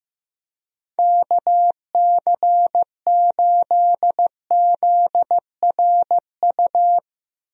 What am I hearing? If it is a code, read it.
Morse code, heard as a keyed tone, about 15 wpm, 700 Hz. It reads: KC8ZRU
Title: USA Call Signs - 6 Characters - Repeated 3x - 15wpm